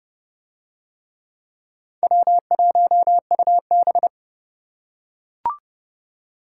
Morse code, heard as a keyed tone, about 29 wpm, 700 Hz. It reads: W1UB